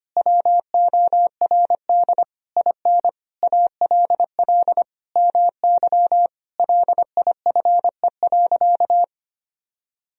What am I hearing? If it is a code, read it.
Morse code, heard as a keyed tone, about 25 wpm, 700 Hz. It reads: WORD IN ALL MY LIFE.
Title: Sentences from Top 1000 Words - Review Part 1 - 25wpm